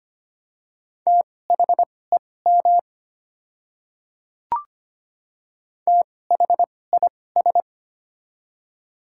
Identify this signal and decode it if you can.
Morse code, heard as a keyed tone, about 25 wpm, spaced as for 18 wpm, 700 Hz. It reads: THEM THIS
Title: 2x Character Spacing - Top 100 Words - 25wpm